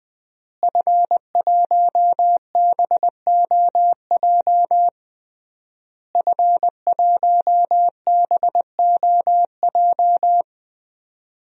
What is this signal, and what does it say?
Morse code, heard as a keyed tone, about 21 wpm, 700 Hz. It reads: F1BOJ F1BOJ
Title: DXCC-100 Call Signs - 5 Characters - Repeated 3x - 20wpm